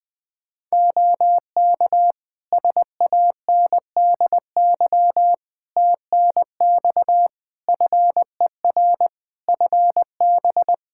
Morse code, heard as a keyed tone, about 20 wpm, 700 Hz. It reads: OK SANDY TNX FER FB
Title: K1USN SST QSO Practice 20wpm